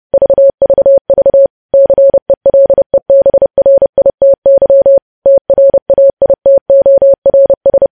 Morse code, heard as a keyed tone, about 30 wpm, 550 Hz. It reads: VVV CELEBRITY TRAITORS